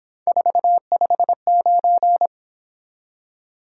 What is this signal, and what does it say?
Morse code, heard as a keyed tone, about 26 wpm, 700 Hz. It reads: 459
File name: Speed-Racing - Three Letter-Number Combinations - 22wpm